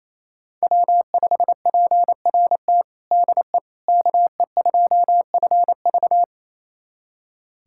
Morse code, heard as a keyed tone, about 28 wpm, 690 Hz. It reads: W5PRT DE KE2FV
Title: USA Callsign DE Callsign - 5 Characters - 28wpm